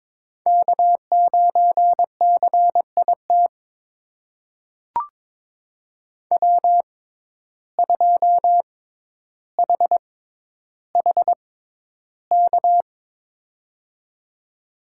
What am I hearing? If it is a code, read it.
Morse code, heard as a keyed tone, about 22 wpm, spaced as for 11 wpm, 700 Hz. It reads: K9CIT W2HHK
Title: USA Call Signs - 6x Character Spacing - 22wpm